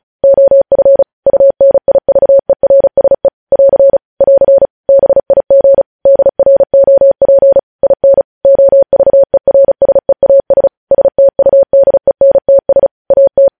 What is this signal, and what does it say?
Morse code, heard as a keyed tone, about 36 wpm, 550 Hz. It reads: OF UNIVERSE <AR> <AR> BIG DROP IN OVERSEAS STUDENTS AT
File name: podcast / News headlines-1x-35x35